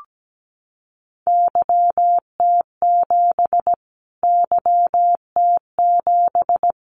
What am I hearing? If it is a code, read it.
Morse code, heard as a keyed tone, about 17 wpm, 700 Hz. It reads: YT7 YT7
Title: Callsign Prefixes for Top 25 DXCC - Repeated 3x - 17wpm